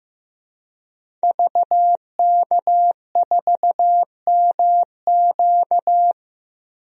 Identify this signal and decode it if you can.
Morse code, heard as a keyed tone, about 15 wpm, 700 Hz. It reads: VK4MQ
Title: DXCC-100 Call Signs - 5 Characters - 15wpm